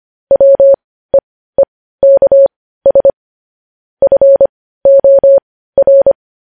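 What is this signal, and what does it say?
Morse code, heard as a keyed tone, about 25 wpm, spaced as for 15 wpm, 550 Hz. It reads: WEEKS FOR
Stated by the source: Morse code news headlines sent at 15.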